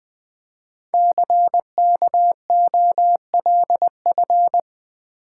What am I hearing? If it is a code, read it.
Morse code, heard as a keyed tone, about 20 wpm, 700 Hz. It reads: CKOLF
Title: Code Groups - 5 Letters - 20wpm